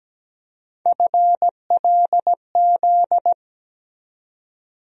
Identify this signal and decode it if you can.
Morse code, heard as a keyed tone, about 17 wpm, 700 Hz. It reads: FLZ
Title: USA Call Sign Postfixes - 3 Letters 17wpm